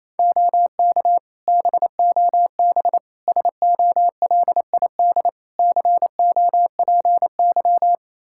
Morse code, heard as a keyed tone, about 28 wpm, 700 Hz. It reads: OK BOB SOLID COPY
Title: Practice QSO - Exchange 3 - More Info - 28wpm